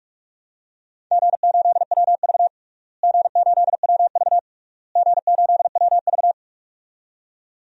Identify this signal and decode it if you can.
Morse code, heard as a keyed tone, about 42 wpm, 700 Hz. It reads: G8WV G8WV G8WV
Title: DXCC-100 Call Signs - 4 Characters - Repeated 3x - 45wpm